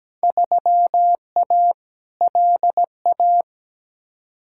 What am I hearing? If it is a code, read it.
Morse code, heard as a keyed tone, about 17 wpm, 700 Hz. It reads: 3A LA
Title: Field Day Exchanges - 17wpm